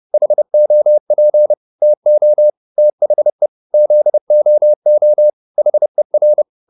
Morse code, heard as a keyed tone, about 30 wpm, 600 Hz. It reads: HOP TO THE ZOO HER